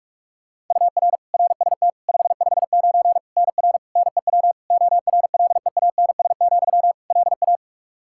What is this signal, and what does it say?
Morse code, heard as a keyed tone, about 41 wpm, 700 Hz. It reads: UR RST 559 NR NEW ORLEANS, LA
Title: K1USN SST QSO Practice 45wpm